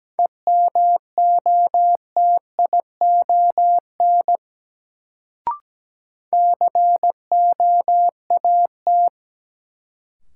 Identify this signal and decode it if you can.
Morse code, heard as a keyed tone, about 17 wpm, 700 Hz. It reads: EMOTION COAT
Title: Sentences from Top 1600 Words - New Word Drill - 17wpm